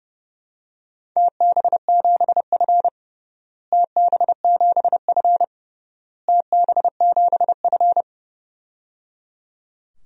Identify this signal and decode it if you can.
Morse code, heard as a keyed tone, about 28 wpm, 700 Hz. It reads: TB7F TB7F TB7F
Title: DXCC-100 Call Signs - 4 Characters - Repeated 3x - 30wpm